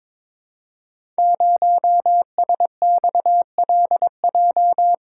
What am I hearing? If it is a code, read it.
Morse code, heard as a keyed tone, about 22 wpm, 700 Hz. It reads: 0SXLJ
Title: Code Groups - 5 Letter:Numbers - 22wpm